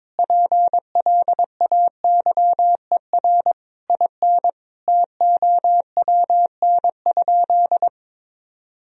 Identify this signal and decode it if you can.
Morse code, heard as a keyed tone, about 22 wpm, 700 Hz. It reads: PLAYER IN TOWN?
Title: Sentences from Top 1000 Words - Review Part 4 - 22wpm